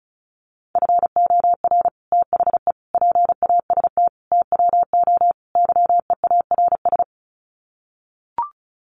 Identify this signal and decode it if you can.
Morse code, heard as a keyed tone, about 34 wpm, 700 Hz. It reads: FOR THE PAST TWO YEARS
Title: Sets of 5 Words Encore2 35wpm